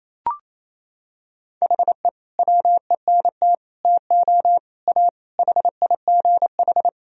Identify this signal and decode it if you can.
Morse code, heard as a keyed tone, about 28 wpm, 700 Hz. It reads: HE WENT TO A HIGH
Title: Sentences from Top 1000 Words - Review Part 1 - 28wpm